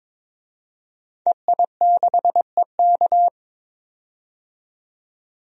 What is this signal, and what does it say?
Morse code, heard as a keyed tone, about 22 wpm, 700 Hz. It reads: EI6EK
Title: DXCC-100 Call Signs - 5 Characters - 22wpm